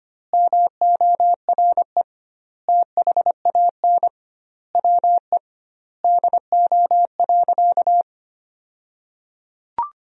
Morse code, heard as a keyed tone, about 25 wpm, 700 Hz. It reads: MORE THAN WE DO.
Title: 2x Word-Spacing - Sentences from Top 100 Words - 25wpm